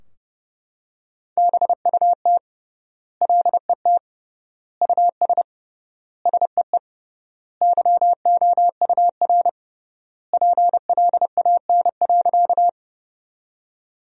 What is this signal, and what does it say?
Morse code, heard as a keyed tone, about 30 wpm, 700 Hz. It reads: BUT LET US SEE YOUR PLAN.
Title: 3x Word-Spacing - Sentences from Top 500 Words - 30wpm